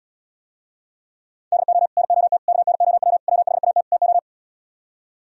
Morse code, heard as a keyed tone, about 78 wpm, 700 Hz. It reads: DO NOT KNOW WHAT TO